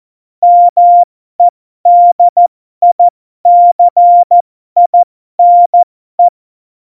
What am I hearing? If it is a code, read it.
Morse code, heard as a keyed tone, about 14 wpm, spaced as for 12 wpm, 700 Hz. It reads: MEDICINE
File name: Morsekurs / Witze